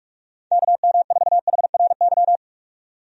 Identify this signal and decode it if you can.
Morse code, heard as a keyed tone, about 43 wpm, 700 Hz. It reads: KM4HRY
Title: USA Call Signs - 6 Characters - 45wpm